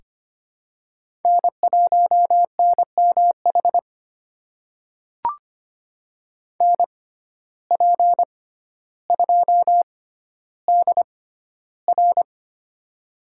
Morse code, heard as a keyed tone, about 25 wpm, spaced as for 13 wpm, 700 Hz. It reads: N1NMH NP2DR
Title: USA Call Signs - 5 Characters - 6x Character Spacing - 25wpm